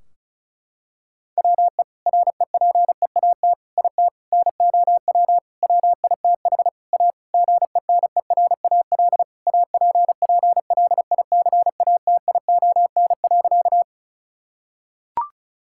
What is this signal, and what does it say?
Morse code, heard as a keyed tone, about 34 wpm, 700 Hz. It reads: WE REPEAT IT NOW WITH A GENERAL APPLICATION.
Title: Sentences from Top 1600 Words 35wpm